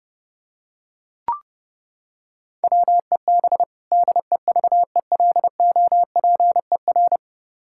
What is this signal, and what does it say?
Morse code, heard as a keyed tone, about 32 wpm, 700 Hz. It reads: WEB DEVELOPER
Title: Practice QSO - Exchange 3 - More Info - 30wpm